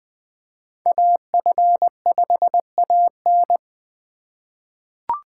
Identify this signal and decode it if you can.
Morse code, heard as a keyed tone, about 20 wpm, 700 Hz. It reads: AF5AN
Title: USA Call Signs - 5 Characters - 4x Character Spacing - 20wpm